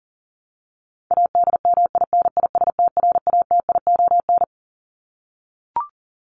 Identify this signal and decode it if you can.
Morse code, heard as a keyed tone, about 40 wpm, 700 Hz. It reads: ADMINISTRATION
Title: Top 1251-1500 Words 40wpm